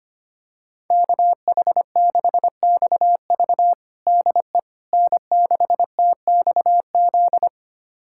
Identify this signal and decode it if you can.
Morse code, heard as a keyed tone, about 25 wpm, 700 Hz. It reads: KH6XV DE N6TXZ
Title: USA Callsign DE Callsign - 5 Characters - 25wpm